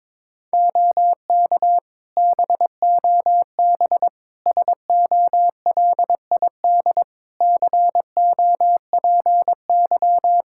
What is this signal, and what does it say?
Morse code, heard as a keyed tone, about 22 wpm, 700 Hz. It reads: OK BOB SOLID COPY
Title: Practice QSO - Exchange 3 - More Info - 22wpm